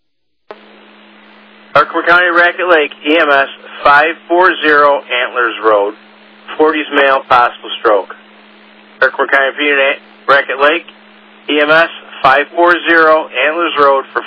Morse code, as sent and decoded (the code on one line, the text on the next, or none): none